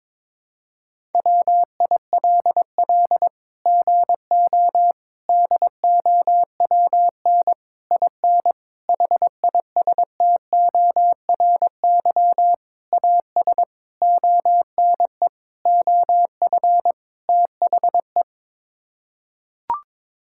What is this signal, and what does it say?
Morse code, heard as a keyed tone, about 21 wpm, 700 Hz. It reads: WILL GO DOWN IN HISTORY AS ONE OF THE
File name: Sets of 9 Words - 22wpm